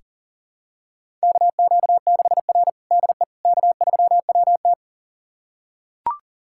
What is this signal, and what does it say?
Morse code, heard as a keyed tone, about 40 wpm, 700 Hz. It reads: KQ6R DE K3WT
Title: USA Callsign DE Callsign - 4 Characters - 40wpm